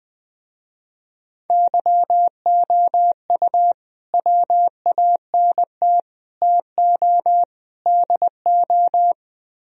Morse code, heard as a keyed tone, about 20 wpm, 700 Hz. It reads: YOU WANT TO DO
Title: Speed-Racing - Sets of 4 Words 17wpm